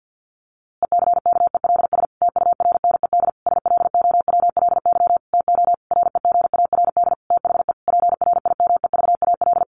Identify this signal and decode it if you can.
Morse code, heard as a keyed tone, about 50 wpm, 700 Hz. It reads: ECKELS TURNED SLOWLY TO REGARD THE PRIMEVAL
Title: A Sound Of Thunder 50wpm